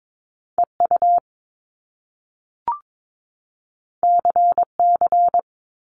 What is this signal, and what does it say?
Morse code, heard as a keyed tone, about 22 wpm, 700 Hz. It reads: EU CC
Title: Top Two Letter Combinations in Words 22wpm